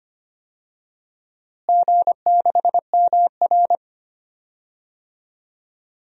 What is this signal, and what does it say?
Morse code, heard as a keyed tone, about 25 wpm, 700 Hz. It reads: G6MR
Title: DXCC-100 Call Signs - 4 Characters - Repeated 2x - 25wpm